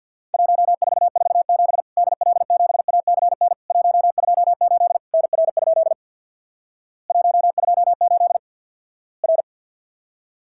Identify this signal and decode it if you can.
Morse code, heard as a keyed tone, about 50 wpm, 700 Hz. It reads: 1447 DL7ACN 128 NR? 128 R
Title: Worked All Europe DX Contest - QTC Exchange with Repeats - 50wpm